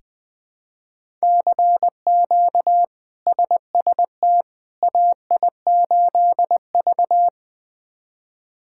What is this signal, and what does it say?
Morse code, heard as a keyed tone, about 20 wpm, 700 Hz. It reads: CQ SST AI8V
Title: K1USN SST QSO Practice - 20wpm